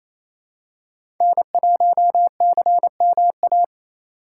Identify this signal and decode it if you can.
Morse code, heard as a keyed tone, about 28 wpm, 700 Hz. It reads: N1CMA
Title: USA Call Signs - 5 Characters - 4x Character Spacing - 28wpm